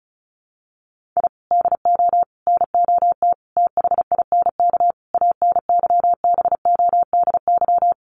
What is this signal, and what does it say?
Morse code, heard as a keyed tone, about 34 wpm, 700 Hz. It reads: I DO NOT THINK ANYBODY
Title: Sets of 5 Words 35wpm